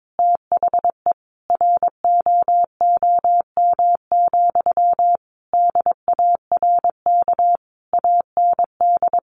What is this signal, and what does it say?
Morse code, heard as a keyed tone, about 22 wpm, 700 Hz. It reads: THE ROOM, DARK AND
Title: Sentences from Top 1400 Words 22wpm